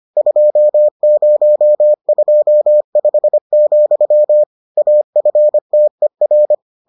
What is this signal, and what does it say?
Morse code, heard as a keyed tone, about 25 wpm, 600 Hz. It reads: 2025, AFTER